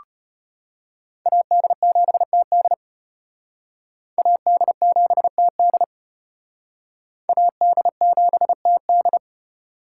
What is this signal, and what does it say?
Morse code, heard as a keyed tone, about 32 wpm, 700 Hz. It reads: AD7TD AD7TD AD7TD